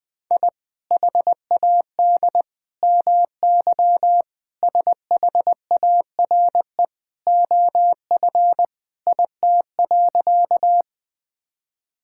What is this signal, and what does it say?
Morse code, heard as a keyed tone, about 20 wpm, 700 Hz. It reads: I HAD MY SHARE OF IT.
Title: Sentences from Top 1000 Words - Review Part 6 - 20wpm